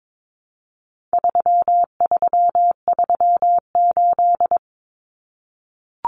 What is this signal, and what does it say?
Morse code, heard as a keyed tone, about 22 wpm, 700 Hz. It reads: 3338
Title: Common Cut Numbers - 4-Digit 22wpm